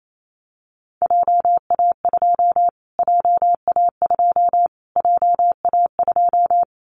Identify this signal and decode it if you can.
Morse code, heard as a keyed tone, about 28 wpm, 700 Hz. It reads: JA2 JA2 JA2
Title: Callsign Prefixes for Top 25 DXCC - Repeated 3x - 28wpm